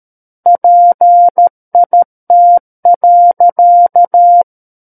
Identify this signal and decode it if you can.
Morse code, heard as a keyed tone, about 13 wpm, 700 Hz. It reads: PIT.
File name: Book One / 04-Chapter 4 - 13 WPM